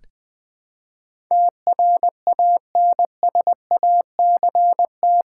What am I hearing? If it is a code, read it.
Morse code, heard as a keyed tone, about 20 wpm, 700 Hz. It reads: TRANSACT
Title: Build A Word - 4 Letters At A Time - 11-Letter Words - 20wpm